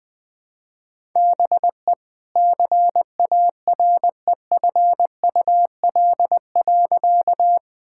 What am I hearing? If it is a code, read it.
Morse code, heard as a keyed tone, about 20 wpm, 700 Hz. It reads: BE CAREFUL.
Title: Sentences from Top 1600 Words 20wpm